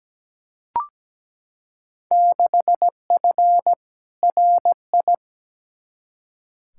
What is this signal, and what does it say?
Morse code, heard as a keyed tone, about 17 wpm, 700 Hz. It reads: 6F RI
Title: Field Day Exchanges Encore - 17wpm